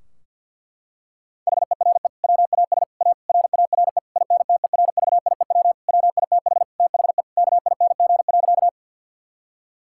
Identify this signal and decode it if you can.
Morse code, heard as a keyed tone, about 50 wpm, 700 Hz. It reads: HERE WAS A RARE INTERVIEW WITH THE KING.
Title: Sentences from Top 1600 Words 50wpm